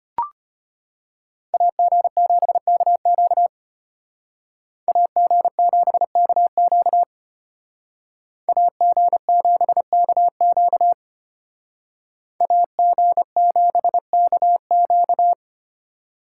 Morse code, heard as a keyed tone, about 31 wpm, 700 Hz. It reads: AG7KQ AG7KQ AG7KQ AG7KQ
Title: Speed Racing - USA 2x2 Call Signs 25wpm